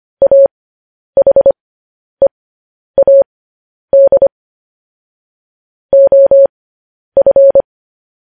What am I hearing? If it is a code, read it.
Morse code, heard as a keyed tone, about 25 wpm, spaced as for 10 wpm, 550 Hz. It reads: AHEAD OF
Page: news headlines in morse code at 10 wpm